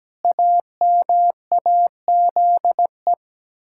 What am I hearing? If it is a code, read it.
Morse code, heard as a keyed tone, about 17 wpm, 700 Hz. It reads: AMAZE
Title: Pangram Challenge v2 - 17wpm